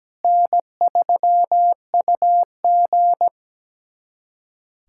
Morse code, heard as a keyed tone, about 17 wpm, 700 Hz. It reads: N3UG
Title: USA Callsign DE Callsign - 4 Characters - 17wpm